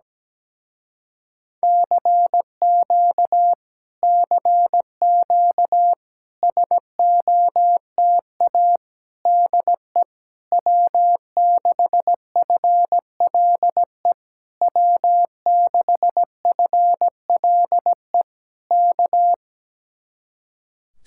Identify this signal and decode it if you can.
Morse code, heard as a keyed tone, about 17 wpm, 700 Hz. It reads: CQ CQ SOTA DE W6FLE W6FLE K
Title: Speed-Racing Common QSO Phrases Encore - 17wpm